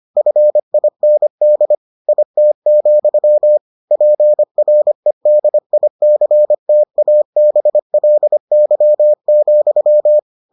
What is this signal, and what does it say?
Morse code, heard as a keyed tone, about 25 wpm, 600 Hz. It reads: FIND IT, PREDICTABLY,